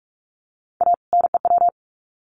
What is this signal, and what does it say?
Morse code, heard as a keyed tone, about 44 wpm, 700 Hz. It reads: A NEW
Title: Sets of 2 Words 45wpm